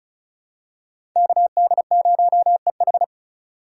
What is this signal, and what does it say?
Morse code, heard as a keyed tone, about 34 wpm, 700 Hz. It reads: KD0EH
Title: USA Call Signs - 5 Characters - 4x Character Spacing - 35wpm